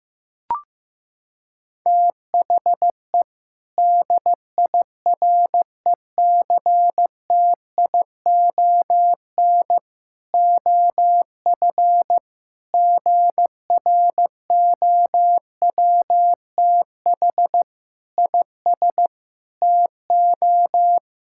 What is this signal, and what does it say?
Morse code, heard as a keyed tone, about 16 wpm, 700 Hz. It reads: THE DIRECTION OF GROWTH IS TO
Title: Sentences from Top 1000 Words - Review Part 5 - 15wpm